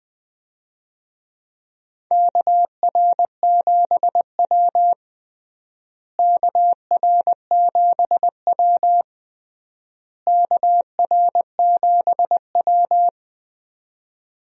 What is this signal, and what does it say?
Morse code, heard as a keyed tone, about 20 wpm, 700 Hz. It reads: KR7W KR7W KR7W
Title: USA Call Signs - 4 Characters - Repeated 3x - 20wpm